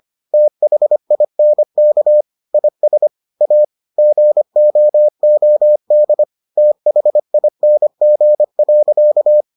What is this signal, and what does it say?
Morse code, heard as a keyed tone, about 25 wpm, 600 Hz. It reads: THINK IS A GOOD THING.